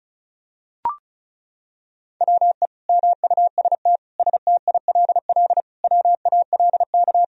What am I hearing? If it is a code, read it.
Morse code, heard as a keyed tone, about 36 wpm, 700 Hz. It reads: WE MUST STILL WALK
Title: Sentences from Top 1000 Words - Review Part 1 - 35wpm